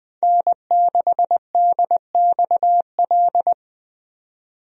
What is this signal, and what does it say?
Morse code, heard as a keyed tone, about 20 wpm, 700 Hz. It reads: N6DXL
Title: USA Call Signs - 5 Characters - 20wpm